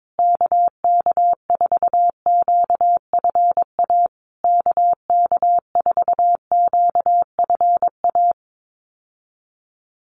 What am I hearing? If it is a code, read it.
Morse code, heard as a keyed tone, about 22 wpm, 700 Hz. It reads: KK4QFA KK4QFA
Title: USA Call Signs Repeated 3x 22wpm